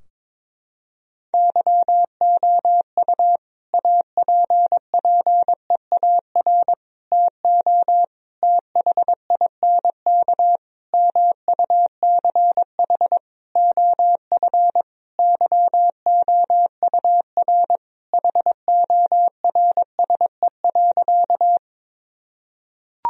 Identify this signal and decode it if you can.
Morse code, heard as a keyed tone, about 22 wpm, 700 Hz. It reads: YOU APPEAR TO THINK MUCH OF YOUR HORSE.
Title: Sentences from Top 1000 Words - Review Part 2 - 22wpm